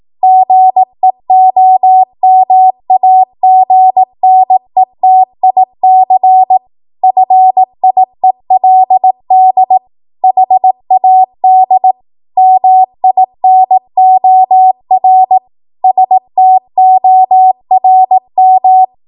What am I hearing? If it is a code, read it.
Morse code, heard as a keyed tone, about 18 wpm, 750 Hz. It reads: GEOMAGNETIC FIELD HAD MINOR STORM
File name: Archive / 18 WPM / 18WPM